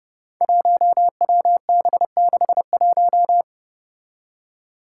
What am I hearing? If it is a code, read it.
Morse code, heard as a keyed tone, about 30 wpm, 700 Hz. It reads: 1WB61
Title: Five Challenging Letter-Number Combinations 30wpm